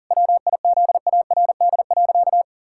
Morse code, heard as a keyed tone, about 40 wpm, 700 Hz. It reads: WIZARD.
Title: Pangram Challenge v2 - 40wpm